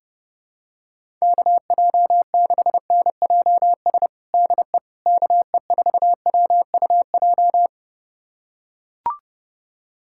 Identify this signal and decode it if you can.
Morse code, heard as a keyed tone, about 32 wpm, 700 Hz. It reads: KJ6NJS DE KE4WUJ
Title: USA Callsign DE Callsign 30wpm